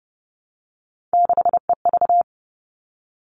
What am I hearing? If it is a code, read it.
Morse code, heard as a keyed tone, about 30 wpm, 700 Hz. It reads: 6EV
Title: All Cut Numbers - 3-Digit 30wpm